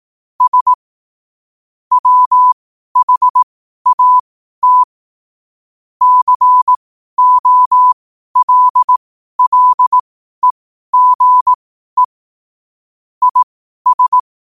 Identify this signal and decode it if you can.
Morse code, heard as a keyed tone, about 18 wpm, spaced as for 12 wpm, 1000 Hz. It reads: S WHAT COLLEGE IS
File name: qotdfiles / 12wpm